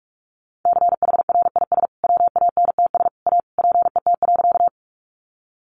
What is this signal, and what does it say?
Morse code, heard as a keyed tone, about 44 wpm, 700 Hz. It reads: CHRIS WANTS A PET.
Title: Additional Easy Phrases 45wpm